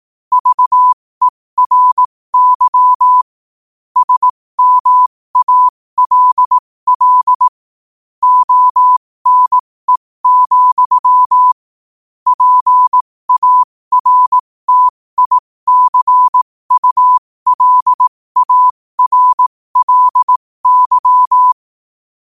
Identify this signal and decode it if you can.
Morse code, heard as a keyed tone, about 18 wpm, spaced as for 15 wpm, 1000 Hz. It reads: VERY SMALL ONE, PARTICULARLY
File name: 15wpm